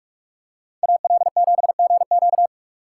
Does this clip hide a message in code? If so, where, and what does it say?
Yes: Morse code from 0.8 to 2.5 s, heard as a keyed tone, about 42 wpm, 700 Hz. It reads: AL7GQ